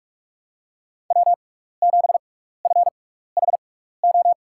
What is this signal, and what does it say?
Morse code, heard as a keyed tone, about 44 wpm, spaced as for 15 wpm, 700 Hz. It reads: W7FHO
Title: USA Call Signs - 5 Characters - 6x Character Spacing - 45wpm